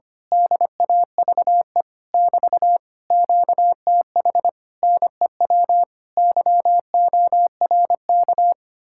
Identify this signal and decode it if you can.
Morse code, heard as a keyed tone, about 24 wpm, 700 Hz. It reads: DAVE <BT> QTH NEW YORK
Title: Speed-Racing Common QSO Phrases - 25wpm